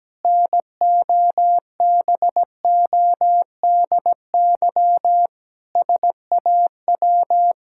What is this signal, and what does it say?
Morse code, heard as a keyed tone, about 17 wpm, 700 Hz. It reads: NOBODY SAW
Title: Sentences from Top 1600 Words 17wpm